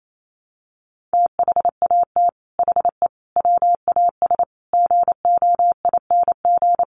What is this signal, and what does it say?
Morse code, heard as a keyed tone, about 28 wpm, 700 Hz. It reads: THAT HE WAS GOING
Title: Sets of 5 Words 28wpm